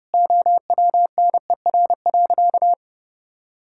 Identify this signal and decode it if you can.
Morse code, heard as a keyed tone, about 32 wpm, 700 Hz. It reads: OWNER.
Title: Sentences from Top 1600 Words 30wpm